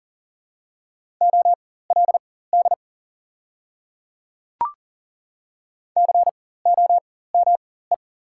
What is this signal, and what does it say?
Morse code, heard as a keyed tone, about 40 wpm, spaced as for 19 wpm, 700 Hz. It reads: OLD COME
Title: 4x Character Spacing - Top 200 Words - 40wpm